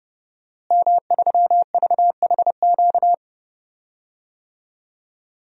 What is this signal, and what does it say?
Morse code, heard as a keyed tone, about 30 wpm, 700 Hz. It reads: M3VHQ
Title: DXCC-100 Call Signs - 5 Characters - Repeated 2x - 30wpm